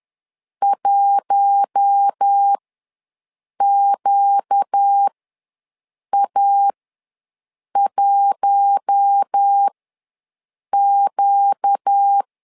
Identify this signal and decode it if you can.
Morse code, heard as a keyed tone, about 10 wpm, spaced as for 6 wpm, 800 Hz. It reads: 1QA1Q
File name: LEZIONI / 4 Q1 ws 10wpm 2538